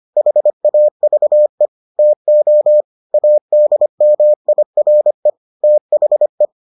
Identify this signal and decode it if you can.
Morse code, heard as a keyed tone, about 25 wpm, 600 Hz. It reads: HAVE TO ADMIRE THE